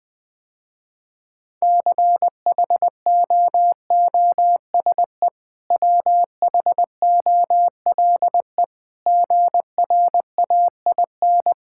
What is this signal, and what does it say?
Morse code, heard as a keyed tone, about 20 wpm, 700 Hz. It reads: CHOOSE WHOLE GRAIN
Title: Easy Sentences with Hints - 20wpm